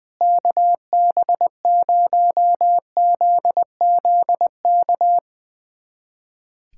Morse code, heard as a keyed tone, about 21 wpm, 700 Hz. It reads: KB0ZZK
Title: USA Callsign DE Callsign - 6 Characters - 20wpm